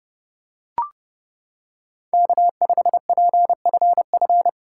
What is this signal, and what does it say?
Morse code, heard as a keyed tone, about 28 wpm, 700 Hz. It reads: K5PFF